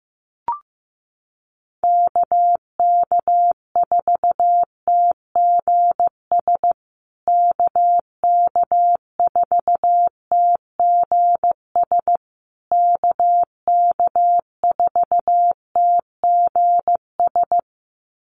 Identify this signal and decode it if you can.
Morse code, heard as a keyed tone, about 15 wpm, 700 Hz. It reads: KK4TGS KK4TGS KK4TGS
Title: USA Call Signs Repeated 3x 15wpm